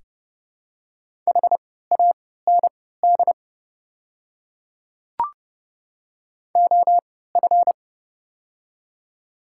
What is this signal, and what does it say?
Morse code, heard as a keyed tone, about 31 wpm, spaced as for 17 wpm, 690 Hz. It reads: HAND OF
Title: 3x Character Spacing - Top 200 Words - 30wpm